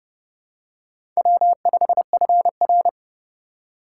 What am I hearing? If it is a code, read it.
Morse code, heard as a keyed tone, about 29 wpm, 700 Hz. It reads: W5FR
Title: USA Call Signs - 4 Characters - 4x Character Spacing - 30wpm